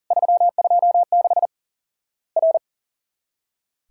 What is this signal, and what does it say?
Morse code, heard as a keyed tone, about 40 wpm, 700 Hz. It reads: 326 R
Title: Worked All Europe DX Contest - QTC Exchange - 40wpm